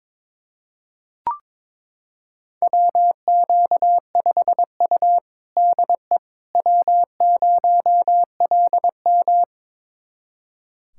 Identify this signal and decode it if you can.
Morse code, heard as a keyed tone, about 22 wpm, 700 Hz. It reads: WQ5U DE W0LM
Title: USA Callsign DE Callsign - 4 Characters - 22wpm